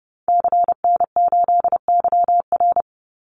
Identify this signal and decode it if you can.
Morse code, heard as a keyed tone, about 30 wpm, 700 Hz. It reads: CN8YR
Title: Call Signs Repeated 2x - 30wpm